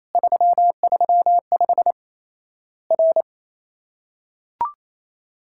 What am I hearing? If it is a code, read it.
Morse code, heard as a keyed tone, about 28 wpm, 700 Hz. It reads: 335 R
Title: Worked All Europe DX Contest - QTC Exchange - 28wpm